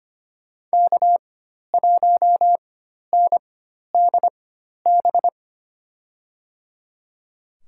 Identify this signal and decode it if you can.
Morse code, heard as a keyed tone, about 25 wpm, spaced as for 12 wpm, 700 Hz. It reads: K1NDB
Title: USA Call Signs - 5 Characters - 4x Character Spacing - 25wpm